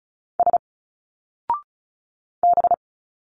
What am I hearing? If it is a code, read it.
Morse code, heard as a keyed tone, about 35 wpm, 700 Hz. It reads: S B
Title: Single Letters 35wpm